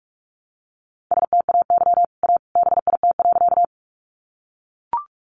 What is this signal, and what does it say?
Morse code, heard as a keyed tone, about 44 wpm, 700 Hz. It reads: STAY A BIT.
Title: Sentences from Top 900 Words 45wpm